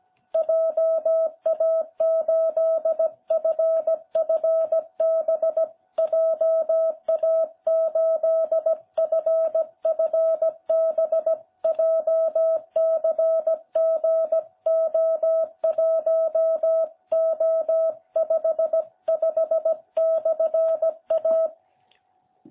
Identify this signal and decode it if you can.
Morse code, heard as a keyed tone, about 16 wpm, 660 Hz. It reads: JA8FFB JA8FFB JCGO1O55/A